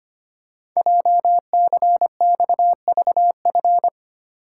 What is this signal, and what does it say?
Morse code, heard as a keyed tone, about 25 wpm, 700 Hz. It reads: JCXVF
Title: Code Groups - 5 Letters - 25wpm